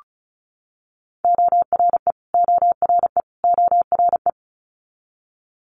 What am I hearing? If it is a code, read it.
Morse code, heard as a keyed tone, about 34 wpm, 700 Hz. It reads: ORE ORE ORE
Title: Multi-letter Phonemes Repeated 3x 35wpm